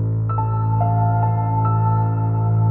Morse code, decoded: none